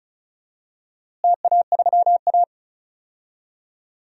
Morse code, heard as a keyed tone, about 36 wpm, 700 Hz. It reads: TA3A
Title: DXCC-100 Call Signs - 4 Characters - Repeated 2x - 35wpm